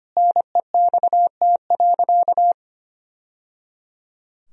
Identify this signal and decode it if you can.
Morse code, heard as a keyed tone, about 25 wpm, 700 Hz. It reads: NEXT.
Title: Sentences from Top 1000 Words - Review Part 2 - 25wpm